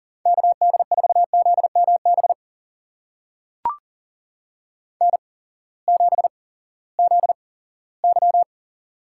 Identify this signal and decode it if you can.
Morse code, heard as a keyed tone, about 40 wpm, spaced as for 20 wpm, 700 Hz. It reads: KD4ZMB N7ZY
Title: USA Call Signs - 8x Character Spacing - 40wpm